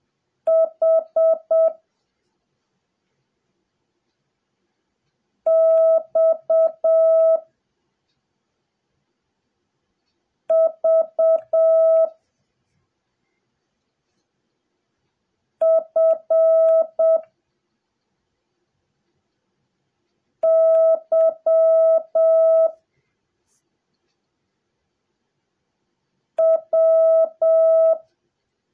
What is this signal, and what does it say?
Morse code, heard as a keyed tone, about 7 wpm, 660 Hz. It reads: H X V F Y W
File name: jh8cbh / 2021cw / cw08a